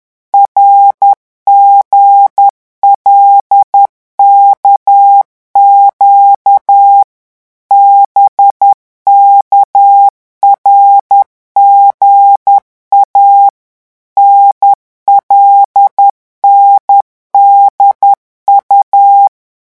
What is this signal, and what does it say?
Morse code, heard as a keyed tone, about 10 wpm, 800 Hz. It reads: RGLKQ BKRGA NLNDU
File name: morse mp3 / revision5-vitesse 10 mots